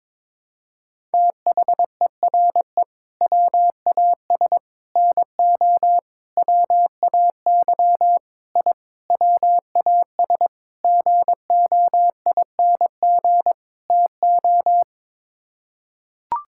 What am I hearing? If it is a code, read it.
Morse code, heard as a keyed tone, about 22 wpm, 700 Hz. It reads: THERE WAS NO WAY I WAS GOING TO
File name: practice / 22wpm / Sets of 8 Words - 22wpm